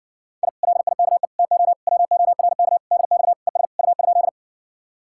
Morse code, heard as a keyed tone, about 60 wpm, 700 Hz. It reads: I LIKE TO WORK DX ES U?